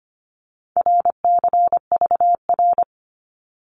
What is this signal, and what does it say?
Morse code, heard as a keyed tone, about 25 wpm, 700 Hz. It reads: RCVR
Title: Abbreviations 25wpm